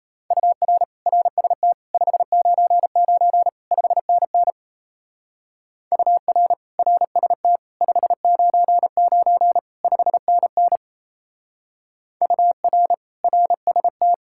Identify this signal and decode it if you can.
Morse code, heard as a keyed tone, about 33 wpm, 700 Hz. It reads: UR RST 599 5NN UR RST 599 5NN UR RST
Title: Speed-Racing Common QSO Phrases - 28wpm